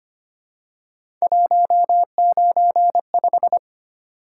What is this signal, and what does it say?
Morse code, heard as a keyed tone, about 29 wpm, 700 Hz. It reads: 195
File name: Speed-Racing - Three Letter-Number Combinations - 25wpm